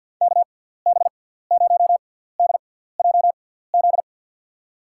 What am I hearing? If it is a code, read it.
Morse code, heard as a keyed tone, about 52 wpm, spaced as for 17 wpm, 700 Hz. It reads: KB0DJZ
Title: USA Call Signs - 6 Characters - 6x Character Spacing - 50wpm